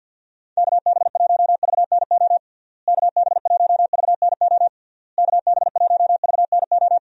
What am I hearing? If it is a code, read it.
Morse code, heard as a keyed tone, about 51 wpm, 700 Hz. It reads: KB1VNO KB1VNO KB1VNO
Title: USA Call Signs - 6 Characters - Repeated 3x - 50wpm